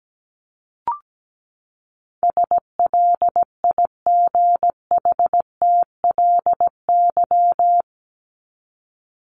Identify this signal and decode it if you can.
Morse code, heard as a keyed tone, about 17 wpm, 700 Hz. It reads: SLIGHTLY
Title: Top 1251-1500 Words 17wpm